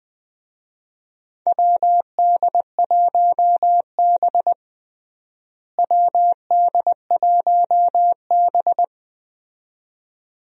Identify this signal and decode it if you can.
Morse code, heard as a keyed tone, about 20 wpm, 700 Hz. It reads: WD1B WD1B
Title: USA Call Signs - 4 Characters - Repeated 2x - 20wpm